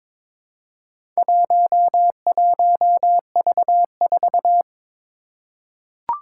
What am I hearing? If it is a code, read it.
Morse code, heard as a keyed tone, about 23 wpm, 700 Hz. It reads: 11V4
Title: Four Challenging Letter-Number Combinations 22wpm